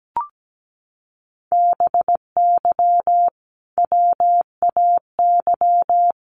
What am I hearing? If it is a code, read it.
Morse code, heard as a keyed tone, about 17 wpm, 700 Hz. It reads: BY WAY
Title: Sentences from Top 1500 Words 17wpm